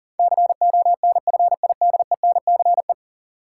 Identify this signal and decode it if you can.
Morse code, heard as a keyed tone, about 40 wpm, 700 Hz. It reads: CONFIDENCE
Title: Sentences from Top 1500 Words - New Word Drill Alt - 40wpm